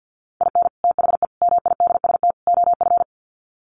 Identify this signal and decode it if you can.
Morse code, heard as a keyed tone, about 50 wpm, 700 Hz. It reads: IN THE MIDST OF
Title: Sets of 4 Words 50wpm